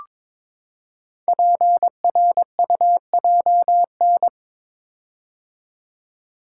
Morse code, heard as a keyed tone, about 22 wpm, 700 Hz. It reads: PRUJN
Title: Code Groups - 5 Letters - 22wpm